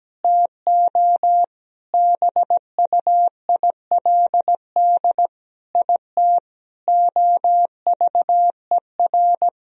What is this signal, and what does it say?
Morse code, heard as a keyed tone, about 17 wpm, 700 Hz. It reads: TO BUILD IT OVER
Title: Sentences from Top 1000 Words - Review Part 1 - 17wpm